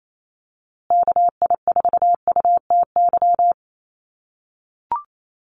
Morse code, heard as a keyed tone, about 28 wpm, 700 Hz. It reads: KI4UTY
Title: USA Call Signs Repeated 2x 28wpm